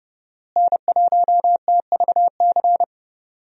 Morse code, heard as a keyed tone, about 30 wpm, 700 Hz. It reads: N1TVC
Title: USA Call Signs - 5 Characters - Repeated 2x - 30wpm